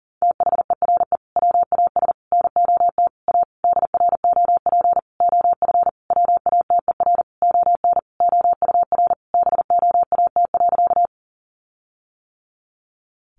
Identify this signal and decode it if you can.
Morse code, heard as a keyed tone, about 40 wpm, 700 Hz. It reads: THERE WAS NOT A DROP OF WATER ON OUR BOAT.
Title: Sentences from Top 1400 Words 40wpm